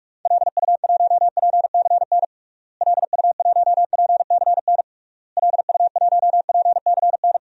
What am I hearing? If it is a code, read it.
Morse code, heard as a keyed tone, about 46 wpm, 700 Hz. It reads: LU1PCN LU1PCN LU1PCN
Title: DXCC-100 Call Signs - 6 Characters - Repeated 3x - 45wpm